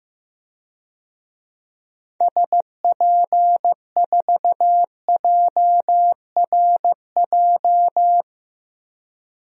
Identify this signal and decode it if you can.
Morse code, heard as a keyed tone, about 16 wpm, 700 Hz. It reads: SP4JRJ
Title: DXCC-100 Call Signs - 6 Characters - 15wpm